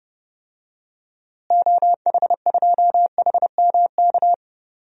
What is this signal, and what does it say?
Morse code, heard as a keyed tone, about 30 wpm, 700 Hz. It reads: OH2HMK
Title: DXCC-100 Call Signs - 6 Characters - Repeated 3x - 30wpm